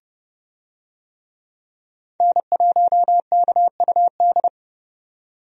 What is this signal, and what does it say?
Morse code, heard as a keyed tone, about 30 wpm, 700 Hz. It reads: N1KUD
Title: USA Call Signs - 5 Characters - 30wpm